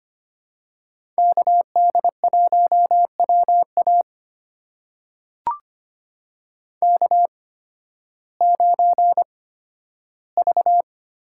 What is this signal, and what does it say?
Morse code, heard as a keyed tone, about 25 wpm, spaced as for 13 wpm, 700 Hz. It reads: KD1WA K9V
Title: USA Call Signs - 5 Characters - 8x Character Spacing - 25wpm